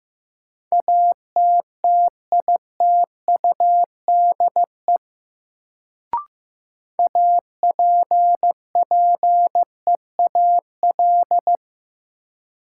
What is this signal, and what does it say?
Morse code, heard as a keyed tone, about 15 wpm, 700 Hz. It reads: ATTITUDE APPEAL
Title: Sentences from Top 1300 Words - New Word Drill Alt - 15wpm